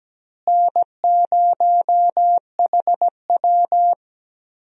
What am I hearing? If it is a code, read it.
Morse code, heard as a keyed tone, about 17 wpm, 700 Hz. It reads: N0HW
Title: USA Call Signs - 4 Characters - Repeated 2x - 17wpm